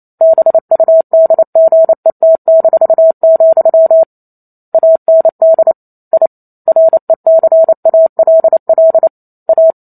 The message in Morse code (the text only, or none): BUDGET-, AND I RECALL A